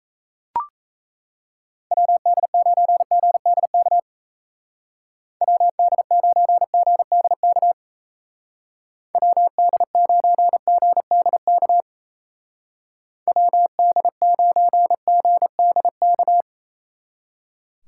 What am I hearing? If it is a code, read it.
Morse code, heard as a keyed tone, about 37 wpm, 700 Hz. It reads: WD9GDK WD9GDK WD9GDK WD9GDK